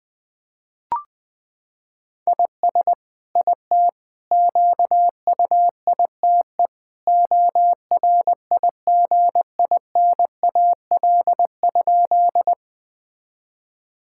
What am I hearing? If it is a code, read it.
Morse code, heard as a keyed tone, about 21 wpm, 700 Hz. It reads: IS IT QUITE ORIGINAL?
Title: Sentences from Top 1000 Words - Review Part 6 - 20wpm